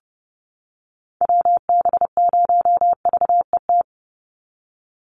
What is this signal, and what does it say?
Morse code, heard as a keyed tone, about 29 wpm, 700 Hz. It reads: WB0VET
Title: USA Call Signs 30wpm